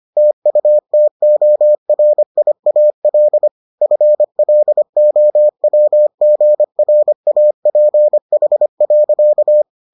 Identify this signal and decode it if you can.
Morse code, heard as a keyed tone, about 25 wpm, 600 Hz. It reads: TUTORIAL FLOWGRAPH.